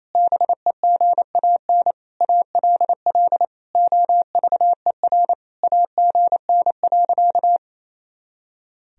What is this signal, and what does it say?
Morse code, heard as a keyed tone, about 28 wpm, 700 Hz. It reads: BEGAN ALL OVER AGN.
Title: Sentences from Top 500 Words and Abbreviations 28wpm